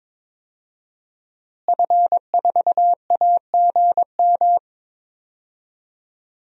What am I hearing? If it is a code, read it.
Morse code, heard as a keyed tone, about 22 wpm, 700 Hz. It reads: F4AGM
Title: DXCC-100 Call Signs - 5 Characters - Repeated 3x - 22wpm